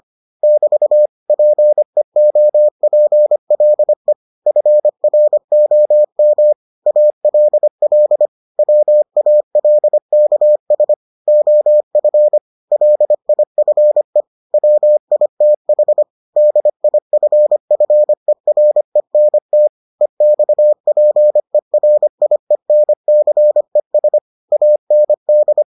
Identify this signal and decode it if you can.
Morse code, heard as a keyed tone, about 25 wpm, 600 Hz. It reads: <BT> PEOPLE FROM ALL WALKS OF LIFE WITH DIFFERENT EXPERIENCES AND